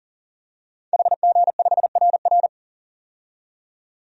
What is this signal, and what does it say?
Morse code, heard as a keyed tone, about 40 wpm, 700 Hz. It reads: HG5RR
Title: DXCC-100 Call Signs - 5 Characters - 40wpm